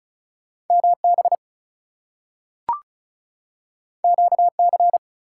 Morse code, heard as a keyed tone, about 36 wpm, 700 Hz. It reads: MB QC
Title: US States and Canadian Provinces 35wpm